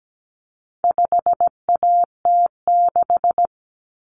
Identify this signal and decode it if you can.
Morse code, heard as a keyed tone, about 17 wpm, 700 Hz. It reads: 5AT6
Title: Common Cut Numbers - 4-Digit 17wpm